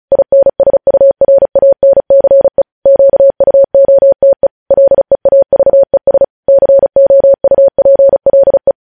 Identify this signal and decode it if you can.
Morse code, heard as a keyed tone, about 36 wpm, 550 Hz. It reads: INSURANCE QUOTE LEAVES COUPLE
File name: podcast / News headlines-1x-35x35